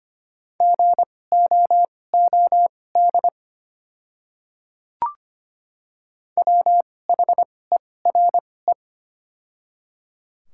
Morse code, heard as a keyed tone, about 25 wpm, spaced as for 18 wpm, 690 Hz. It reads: GOOD WHERE
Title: 2x Character Spacing - Top 200 Words - 25wpm